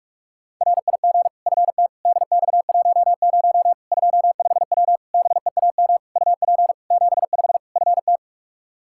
Morse code, heard as a keyed tone, about 42 wpm, 700 Hz. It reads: RIG FT DX10 25W BEAM UP 75 FT
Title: Practice QSO - Exchange 2 - Info - 45wpm